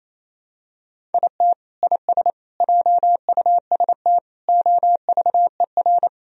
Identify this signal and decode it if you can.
Morse code, heard as a keyed tone, about 28 wpm, 700 Hz. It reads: IT IS JUST OVER
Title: Sentences from Top 1000 Words - Review Part 1 - 28wpm